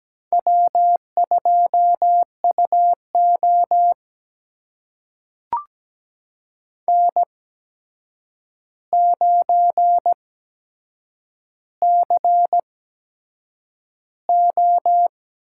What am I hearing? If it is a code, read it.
Morse code, heard as a keyed tone, about 17 wpm, spaced as for 7 wpm, 700 Hz. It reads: W2UO N9CO
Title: USA Call Signs - 4 Characters - 8x Character Spacing - 17wpm